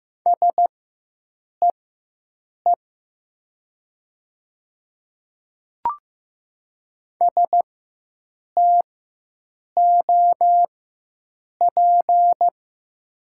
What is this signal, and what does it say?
Morse code, heard as a keyed tone, about 15 wpm, spaced as for 7 wpm, 700 Hz. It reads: SEE STOP
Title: 4x Character Spacing - Top 500 Words - 15wpm